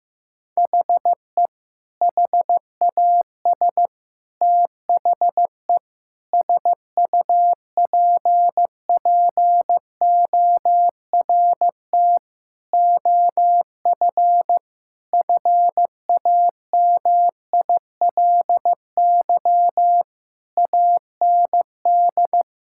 Morse code, heard as a keyed tone, about 15 wpm, 700 Hz. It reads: HE HAS THE SUPPORT OF FAMILY AND
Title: Sentences from Top 1000 Words - Review Part 5 - 15wpm